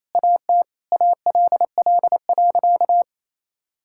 Morse code, heard as a keyed tone, about 28 wpm, 700 Hz. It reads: AT ALL.
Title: Sentences from Top 1000 Words - Review Part 5 - 28wpm